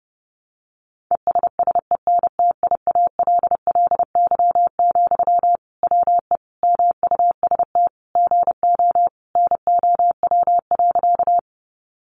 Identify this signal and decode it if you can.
Morse code, heard as a keyed tone, about 30 wpm, 700 Hz. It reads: ESSENTIALLY, WE MUST GO NOW.
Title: Sentences from Top 1300 Words 30wpm